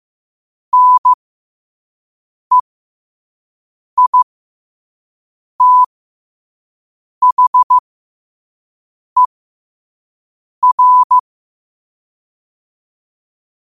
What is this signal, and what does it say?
Morse code, heard as a keyed tone, about 15 wpm, spaced as for 5 wpm, 1000 Hz. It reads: NEITHER
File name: qotdfiles / 5wpm